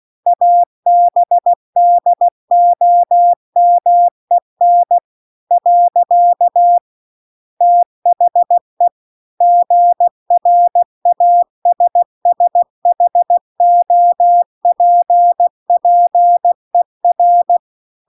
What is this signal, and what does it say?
Morse code, heard as a keyed tone, about 16 wpm, 700 Hz. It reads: ABDOMEN . THE GRASSHOPPER